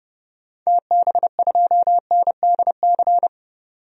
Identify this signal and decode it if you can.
Morse code, heard as a keyed tone, about 30 wpm, 700 Hz. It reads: TB2NDC
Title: DXCC-100 Call Signs - 6 Characters - 30wpm